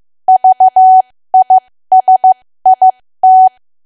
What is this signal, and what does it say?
Morse code, heard as a keyed tone, about 15 wpm, spaced as for 13 wpm, 750 Hz. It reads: VISIT
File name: cwa / W1AW-Level2 / 13WPM